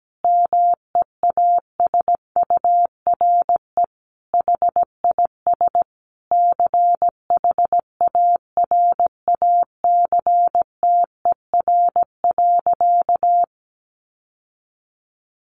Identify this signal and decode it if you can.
Morse code, heard as a keyed tone, about 17 wpm, 700 Hz. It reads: MEASURE HIS CHARACTER.
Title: Sentences from Top 1200 Words 17wpm